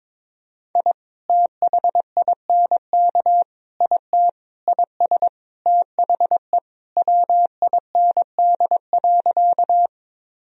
Morse code, heard as a keyed tone, about 22 wpm, 700 Hz. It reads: I THINK IT IS THE WIND.
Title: Sentences from Top 1000 Words - Review Part 2 - 22wpm